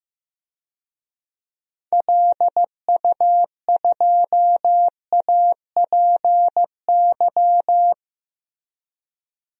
Morse code, heard as a keyed tone, about 16 wpm, 700 Hz. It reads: LU2APY
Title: DXCC-100 Call Signs - 6 Characters - Repeated 2x - 15wpm